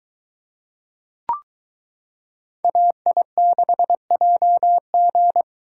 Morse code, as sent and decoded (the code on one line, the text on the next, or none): .- .. -.... .--- --.
AI6JG